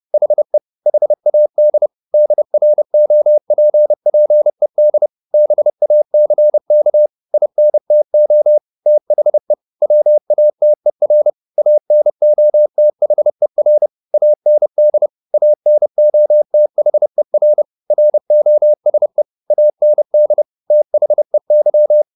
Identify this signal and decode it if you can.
Morse code, heard as a keyed tone, about 30 wpm, 600 Hz. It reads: HE HAD DROPPED BACK INTO THE WATER ANOTHER AND ANOTHER ROSE AND THEY
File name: the old man and the sea / the old man and the sea 1 30wpm